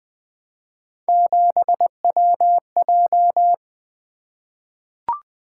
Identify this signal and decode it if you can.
Morse code, heard as a keyed tone, about 20 wpm, 700 Hz. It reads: 7WJ
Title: Three Letter-Number Combinations - 20wpm